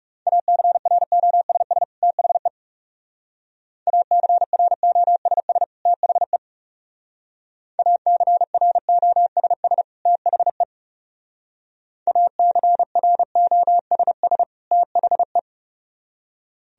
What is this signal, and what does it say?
Morse code, heard as a keyed tone, about 38 wpm, 700 Hz. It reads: ACROSS THE ACROSS THE ACROSS THE ACROSS THE